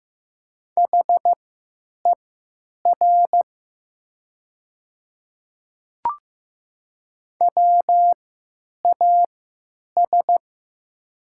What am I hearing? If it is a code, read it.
Morse code, heard as a keyed tone, about 15 wpm, spaced as for 9 wpm, 700 Hz. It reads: HER WAS
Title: 3x Character Spacing - Top 100 Words - 15wpm